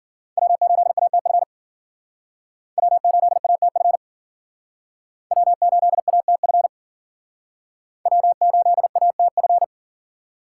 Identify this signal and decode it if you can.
Morse code, heard as a keyed tone, about 51 wpm, 700 Hz. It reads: W8ATF W8ATF W8ATF W8ATF